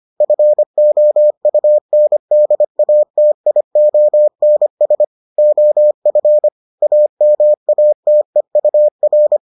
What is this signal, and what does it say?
Morse code, heard as a keyed tone, about 25 wpm, 600 Hz. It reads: FOUNDATIONS OF AMATEUR